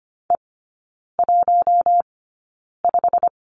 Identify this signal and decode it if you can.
Morse code, heard as a keyed tone, about 25 wpm, spaced as for 9 wpm, 700 Hz. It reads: E15
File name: Lesson 15e - Number 5 - Three Characters 25wpm